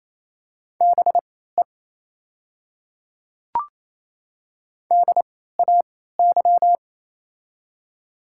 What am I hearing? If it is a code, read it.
Morse code, heard as a keyed tone, about 27 wpm, spaced as for 16 wpm, 700 Hz. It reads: BE DAY
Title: 3x Character Spacing - Top 100 Words - 28wpm